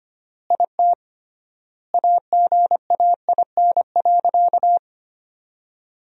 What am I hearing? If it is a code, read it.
Morse code, heard as a keyed tone, about 25 wpm, 700 Hz. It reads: IT AGAIN.
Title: 3x Word-Spacing - Sentences from Top 300 Words - 25wpm